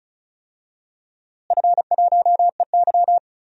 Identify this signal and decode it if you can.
Morse code, heard as a keyed tone, about 34 wpm, 700 Hz. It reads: F1EY
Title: DXCC-100 Call Signs - 4 Characters - Repeated 3x - 35wpm